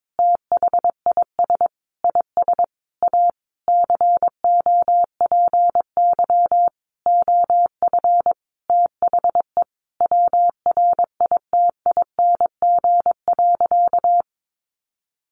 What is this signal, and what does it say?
Morse code, heard as a keyed tone, about 22 wpm, 700 Hz. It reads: THIS IS A COPY OF THE WRITING.
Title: Sentences from Top 1400 Words 22wpm